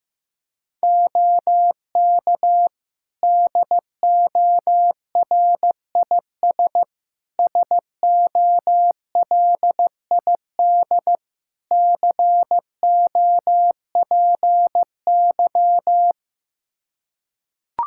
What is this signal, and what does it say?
Morse code, heard as a keyed tone, about 15 wpm, 700 Hz. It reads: OK DORIS SOLID COPY
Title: Practice QSO - Exchange 3 - More Info - 15wpm